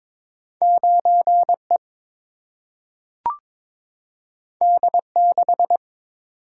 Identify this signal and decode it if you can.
Morse code, heard as a keyed tone, about 22 wpm, 700 Hz. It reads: 9 E D 6
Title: Single-Letter Single-Number 22wpm